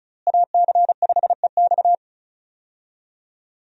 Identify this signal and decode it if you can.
Morse code, heard as a keyed tone, about 36 wpm, 700 Hz. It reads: AC5EX
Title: USA Call Signs - 5 Characters - Repeated 2x - 35wpm